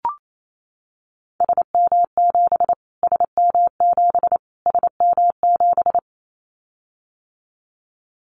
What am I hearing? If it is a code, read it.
Morse code, heard as a keyed tone, about 28 wpm, 700 Hz. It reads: SM7 SM7 SM7
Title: Callsign Prefixes for Top 25 DXCC - Repeated 3x - 28wpm